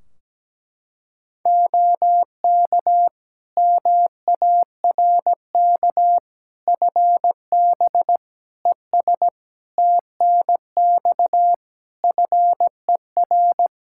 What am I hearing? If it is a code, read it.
Morse code, heard as a keyed tone, about 17 wpm, 700 Hz. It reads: OK MARK FB ES TNX FER
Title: Practice QSO - Exchange 2 - Info - 17wpm